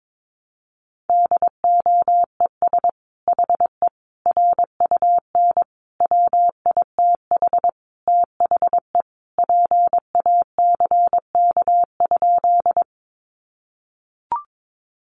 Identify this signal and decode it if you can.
Morse code, heard as a keyed tone, about 22 wpm, 700 Hz. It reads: DOES HE RUN WITH THE PACK?
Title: Sentences from Top 1300 Words 22wpm